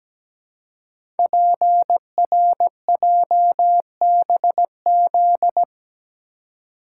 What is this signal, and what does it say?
Morse code, heard as a keyed tone, about 17 wpm, 700 Hz. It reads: PRJBZ
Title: Code Groups - 5 Letters - 17wpm